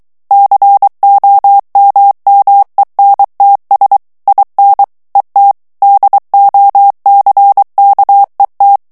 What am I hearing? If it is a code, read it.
Morse code, heard as a keyed tone, about 23 wpm, 800 Hz. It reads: COMMENTS IN ET DOCKET
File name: ANGLAIS texte clair 3 vitesse 22 mots minute